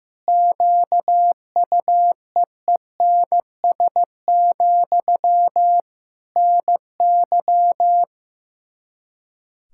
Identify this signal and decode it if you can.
Morse code, heard as a keyed tone, about 15 wpm, 700 Hz. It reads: QUEENS, NY
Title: Practice QSO - Exchange 1 - Beginning - 15wpm